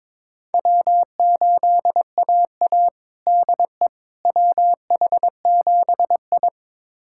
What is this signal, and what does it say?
Morse code, heard as a keyed tone, about 22 wpm, 700 Hz. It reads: W8AA DE WH7I
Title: USA Callsign DE Callsign - 4 Characters - 22wpm